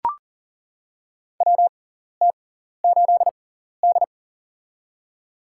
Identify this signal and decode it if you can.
Morse code, heard as a keyed tone, about 40 wpm, spaced as for 14 wpm, 700 Hz. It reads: WT8D
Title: USA Call Signs - 4 Characters - 6x Character Spacing - 40wpm